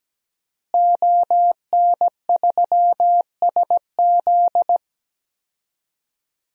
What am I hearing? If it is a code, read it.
Morse code, heard as a keyed tone, about 17 wpm, 700 Hz. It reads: ON3SZ
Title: DXCC-100 Call Signs - 5 Characters - Repeated 3x - 17wpm